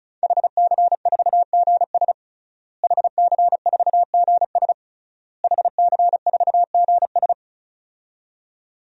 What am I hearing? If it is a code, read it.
Morse code, heard as a keyed tone, about 36 wpm, 700 Hz. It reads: HC4GS HC4GS HC4GS
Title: DXCC-100 Call Signs - 5 Characters - Repeated 3x - 35wpm